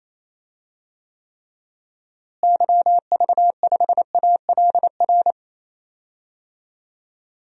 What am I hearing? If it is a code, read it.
Morse code, heard as a keyed tone, about 27 wpm, 700 Hz. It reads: YV5ALR
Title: DXCC-100 Call Signs - 6 Characters - Repeated 2x - 28wpm